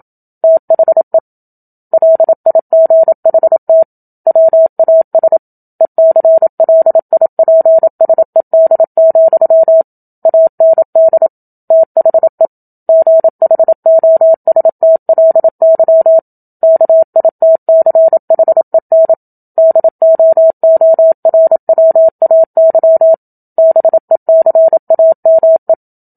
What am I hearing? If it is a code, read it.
Morse code, heard as a keyed tone, about 27 wpm, 650 Hz. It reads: THE LIGHT WAS ECLIPSED, AND THE GHOSTLY KITCHEN DOORWAY BECAME
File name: Book Two / 01-Chapter 1 - 27 WPM